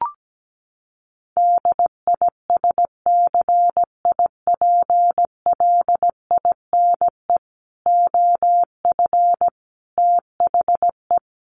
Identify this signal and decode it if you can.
Morse code, heard as a keyed tone, about 17 wpm, 700 Hz. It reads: DISCIPLINE OF THE
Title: Sentences from Top 900 Words 17wpm